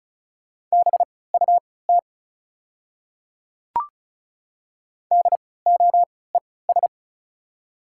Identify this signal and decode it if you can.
Morse code, heard as a keyed tone, about 36 wpm, spaced as for 20 wpm, 700 Hz. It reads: BUT DOES
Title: 3x Character Spacing - Top 500 Words - 35wpm